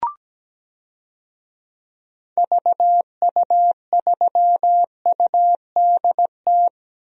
Morse code, heard as a keyed tone, about 17 wpm, 700 Hz. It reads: VU3UDT
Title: DXCC-100 Call Signs - 6 Characters - 17wpm